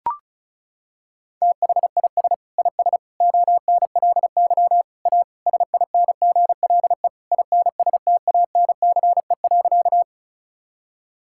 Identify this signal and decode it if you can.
Morse code, heard as a keyed tone, about 34 wpm, 700 Hz. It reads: THIS IS ONLY A SINGLE INSTANCE.
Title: Sentences from Top 1000 Words - Review Part 5 - 35wpm